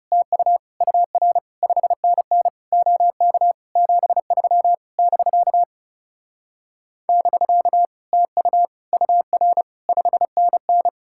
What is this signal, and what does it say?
Morse code, heard as a keyed tone, about 40 wpm, 700 Hz. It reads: TU UR 5NN OK 73 <BK> <BK> TU UR 5NN